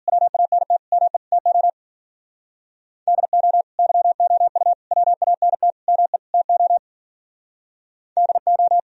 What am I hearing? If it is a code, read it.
Morse code, heard as a keyed tone, about 50 wpm, 700 Hz. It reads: WANT ME TO DO YOU WANT ME TO DO